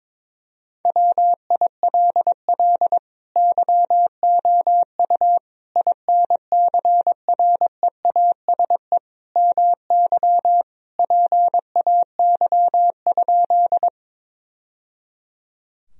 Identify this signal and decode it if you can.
Morse code, heard as a keyed tone, about 22 wpm, 700 Hz. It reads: WILL YOU INCREASE MY PAY?
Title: Sentences from Top 1000 Words - Review Part 4 - 22wpm